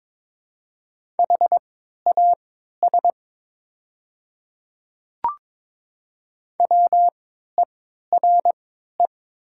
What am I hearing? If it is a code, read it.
Morse code, heard as a keyed tone, about 22 wpm, spaced as for 12 wpm, 700 Hz. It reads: HAS WERE
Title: 3x Character Spacing - Top 100 Words - 22wpm